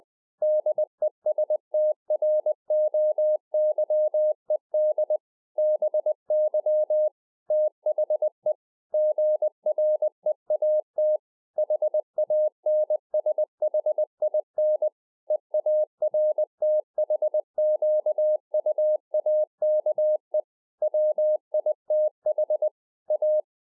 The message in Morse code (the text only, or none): DESTROYED BY THE GREAT HANSHIN EARTHQUAKE WITH A